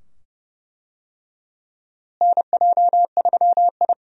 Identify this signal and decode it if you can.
Morse code, heard as a keyed tone, about 31 wpm, 700 Hz. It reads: NJ3I
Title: USA Call Signs - 4 Characters - Repeated 2x - 30wpm